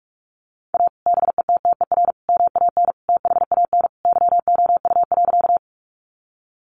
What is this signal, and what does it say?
Morse code, heard as a keyed tone, about 44 wpm, 700 Hz. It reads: A BETTER MAN THAN YOU.
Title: Sentences from Top 400 Words 45wpm